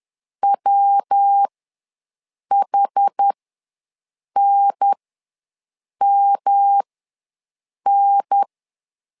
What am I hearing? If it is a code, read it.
Morse code, heard as a keyed tone, about 10 wpm, spaced as for 6 wpm, 800 Hz. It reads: WHNMN
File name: LEZIONI / 3 MWRO / 3 MW ws 10wpm 2213